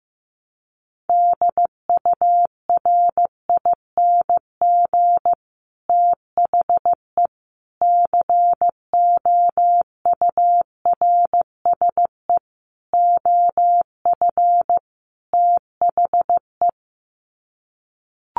Morse code, heard as a keyed tone, about 14 wpm, 700 Hz. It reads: DURING THE COURSE OF THE
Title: Sets of 5 Words Encore 15wpm